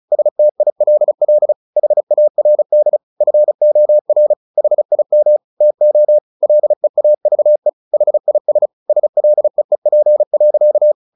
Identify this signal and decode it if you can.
Morse code, heard as a keyed tone, about 34 wpm, 600 Hz. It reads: STILL HARD FOR HIM TO LEAVE HIS SLEEP.
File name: the old man and the sea / the old man and the sea 1 35wpm